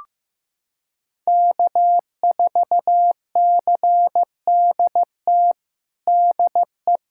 Morse code, heard as a keyed tone, about 15 wpm, 700 Hz. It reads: K4CDT DE
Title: USA Callsign DE Callsign - 5 Characters - 15wpm